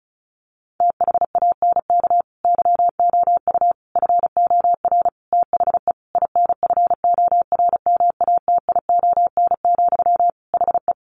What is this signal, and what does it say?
Morse code, heard as a keyed tone, about 34 wpm, 700 Hz. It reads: THANK YOU FOR THE INFORMATION, HE
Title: Sentences from Top 600 Words 35wpm